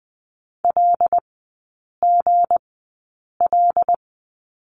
Morse code, heard as a keyed tone, about 20 wpm, spaced as for 8 wpm, 690 Hz. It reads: LGL
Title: 172 Lesson 28e - Number 9 - Three Characters 20wpm